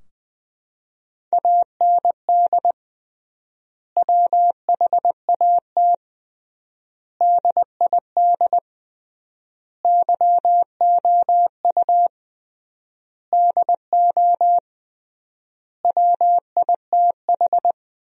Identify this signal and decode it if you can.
Morse code, heard as a keyed tone, about 20 wpm, 700 Hz. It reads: AND WHAT DID YOU DO WITH
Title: 3x Word-Spacing - Sentences from Top 300 Words - 20wpm